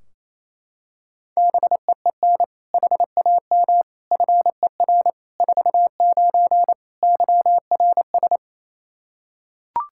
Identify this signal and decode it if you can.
Morse code, heard as a keyed tone, about 28 wpm, 700 Hz. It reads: BEEN HAM FER 49 YRS
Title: Practice QSO - Exchange 3 - More Info - 28wpm